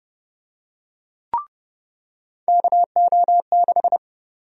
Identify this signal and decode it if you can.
Morse code, heard as a keyed tone, about 30 wpm, 700 Hz. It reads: KO6
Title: USA Call Sign Prefixes - 30wpm